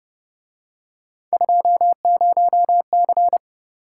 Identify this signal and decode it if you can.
Morse code, heard as a keyed tone, about 31 wpm, 700 Hz. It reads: 20C
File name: Speed-Racing - Three Letter-Number Combinations - 25wpm